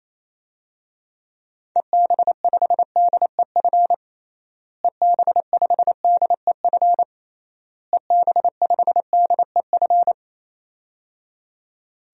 Morse code, heard as a keyed tone, about 28 wpm, 700 Hz. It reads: EB5DEF EB5DEF EB5DEF
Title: DXCC-100 Call Signs - 6 Characters - Repeated 3x - 28wpm